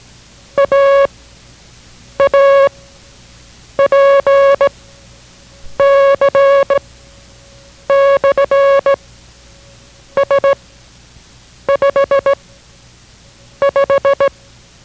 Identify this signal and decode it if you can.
Morse code, heard as a keyed tone, about 17 wpm, 560 Hz. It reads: A A P C / S 5 5